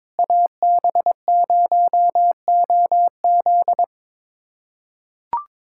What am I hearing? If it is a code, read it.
Morse code, heard as a keyed tone, about 22 wpm, 700 Hz. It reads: AB0OZ
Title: USA Callsign DE Callsign - 5 Characters - 22wpm